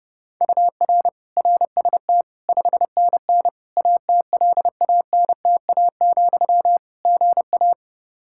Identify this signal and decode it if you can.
Morse code, heard as a keyed tone, about 32 wpm, 700 Hz. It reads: UR RST 5NN ATLANTA, GA
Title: Practice QSO - Exchange 1 - Beginning - 30wpm